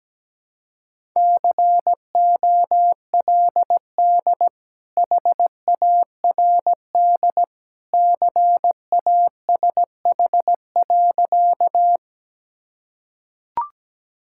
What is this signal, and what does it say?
Morse code, heard as a keyed tone, about 17 wpm, 700 Hz. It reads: COLD HARD CASH.
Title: Sentences from Top 1000 Words - Review Part 5 - 17wpm